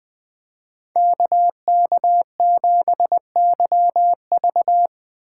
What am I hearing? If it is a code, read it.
Morse code, heard as a keyed tone, about 20 wpm, 700 Hz. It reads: KK7YV
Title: Speed Racing - USA 2x2 Call Signs 17wpm